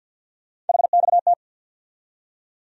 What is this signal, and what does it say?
Morse code, heard as a keyed tone, about 53 wpm, 700 Hz. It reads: HXT
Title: USA Call Sign Postfixes - 3 Letters 50wpm